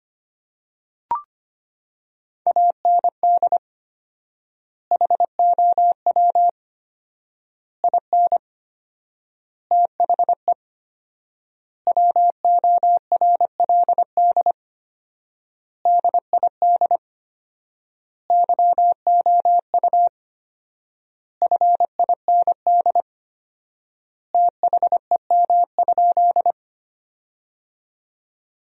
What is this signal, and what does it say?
Morse code, heard as a keyed tone, about 25 wpm, 700 Hz. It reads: AND HOW IN THE WORLD DID YOU FIND THEM?
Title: 4x Word-Spacing - Sentences from Top 200 Words - 25wpm